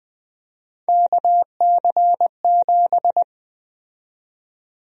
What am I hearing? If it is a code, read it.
Morse code, heard as a keyed tone, about 21 wpm, 700 Hz. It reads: KC7
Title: USA Call Sign Prefixes - 20wpm